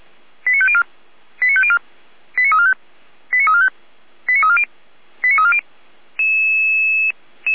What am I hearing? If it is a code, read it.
Five-tone selective call (ZVEI), heard as quick runs of five tones, tones of about 70 ms. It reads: 89504 89504 89345 89345 89340 89340